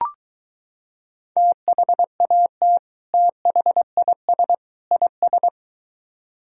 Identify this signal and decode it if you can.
Morse code, heard as a keyed tone, about 23 wpm, 700 Hz. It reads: THAT THIS IS